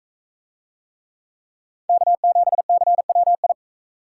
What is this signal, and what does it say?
Morse code, heard as a keyed tone, about 42 wpm, 700 Hz. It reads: K7CWI